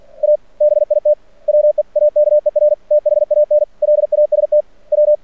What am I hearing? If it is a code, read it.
Morse code, heard as a keyed tone, about 48 wpm, 600 Hz. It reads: A BIT WEAKER THAN LAST W
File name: archived / CRSnet